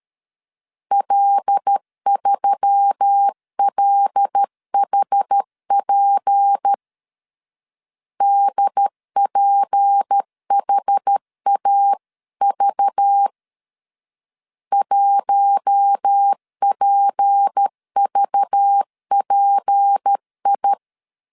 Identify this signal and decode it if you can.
Morse code, heard as a keyed tone, about 13 wpm, 800 Hz. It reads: L3LHP DPHAV 1PVPI
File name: LEZIONI / 6 PKX0 / 6 P 12wpm 2033